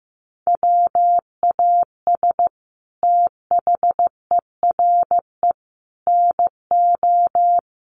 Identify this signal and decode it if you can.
Morse code, heard as a keyed tone, about 15 wpm, 700 Hz. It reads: WAS THERE NO
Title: Sentences from Top 800 Words 15wpm